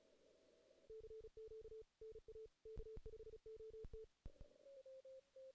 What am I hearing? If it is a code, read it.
Morse code, heard as a keyed tone, about 35 wpm, 440 Hz. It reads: CQ NA K5OT
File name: 2016 JAN CW NAQP / 2016 Jan naqp k5ot - 0-2